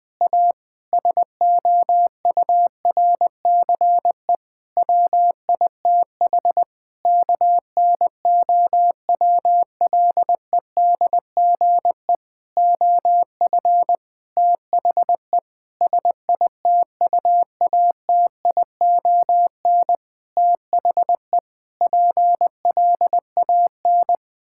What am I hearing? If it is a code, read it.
Morse code, heard as a keyed tone, about 21 wpm, 700 Hz. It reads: A SOURCE WITH KNOWLEDGE OF THE SITUATION THE PLAN
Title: Sentences with 9-Word Phrases - 20wpm